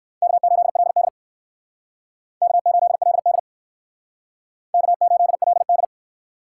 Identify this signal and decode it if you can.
Morse code, heard as a keyed tone, about 58 wpm, 700 Hz. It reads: K8LD K8LD K8LD